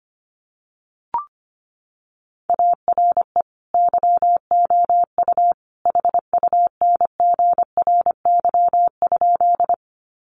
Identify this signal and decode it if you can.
Morse code, heard as a keyed tone, about 25 wpm, 700 Hz. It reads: ARE YOU HUNGRY?
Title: Sentences from Top 1400 Words 25wpm